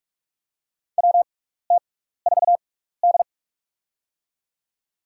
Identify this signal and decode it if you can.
Morse code, heard as a keyed tone, about 44 wpm, spaced as for 15 wpm, 700 Hz. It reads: WT4D
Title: USA Call Signs - 4 Characters - 6x Character Spacing - 45wpm